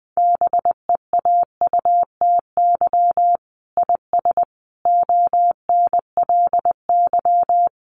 Morse code, heard as a keyed tone, about 20 wpm, 700 Hz. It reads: BEAUTY IS ONLY
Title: Sentences from Top 1300 Words 20wpm